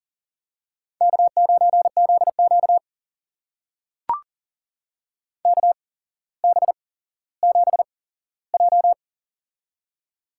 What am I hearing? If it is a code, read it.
Morse code, heard as a keyed tone, about 40 wpm, spaced as for 17 wpm, 700 Hz. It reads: K9ZQ KB7J
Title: USA Call Signs - 8x Character Spacing - 40wpm